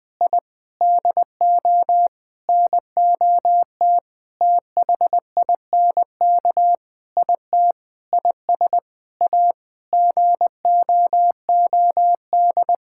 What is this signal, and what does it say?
Morse code, heard as a keyed tone, about 21 wpm, 700 Hz. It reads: I DO NOT THINK IT IS A GOOD
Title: Sentences with 5-Word Phrases - Spoken Beginning - 20wpm